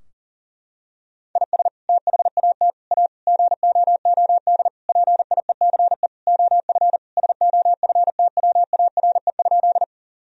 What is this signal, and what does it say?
Morse code, heard as a keyed tone, about 37 wpm, 700 Hz. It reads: IS THAT A GOOD PIECE OF SOFTWARE?
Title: Sentences from Top 1000 Words - Review Part 3 - 40wpm